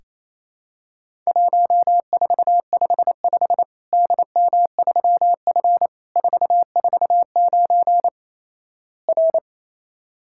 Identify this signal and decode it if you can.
Morse code, heard as a keyed tone, about 27 wpm, 700 Hz. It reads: 1455 DM3F 449 R
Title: Worked All Europe DX Contest - QTC Exchange - 28wpm